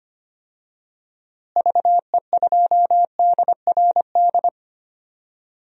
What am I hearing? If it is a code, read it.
Morse code, heard as a keyed tone, about 25 wpm, 700 Hz. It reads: VE2DRD
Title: DXCC-100 Call Signs - 6 Characters - Repeated 3x - 25wpm